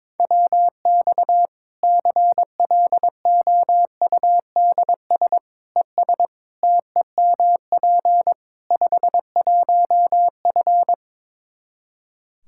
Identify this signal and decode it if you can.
Morse code, heard as a keyed tone, about 22 wpm, 700 Hz. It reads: WX CLOUDS ES TEMP 51F
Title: Practice QSO - Exchange 2 - Info - 22wpm